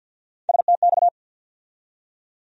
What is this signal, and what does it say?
Morse code, heard as a keyed tone, about 52 wpm, 700 Hz. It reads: STX
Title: USA Call Sign Postfixes - 3 Letters 50wpm